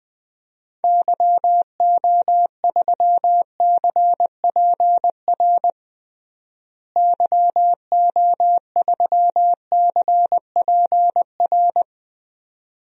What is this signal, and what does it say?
Morse code, heard as a keyed tone, about 19 wpm, 700 Hz. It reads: YO3CPR YO3CPR
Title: DXCC-100 Call Signs - 6 Characters - Repeated 2x - 20wpm